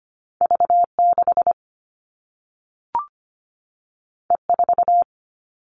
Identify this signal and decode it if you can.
Morse code, heard as a keyed tone, about 25 wpm, 700 Hz. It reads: V6 E4
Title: All Cut Numbers - 2-Digit 25wpm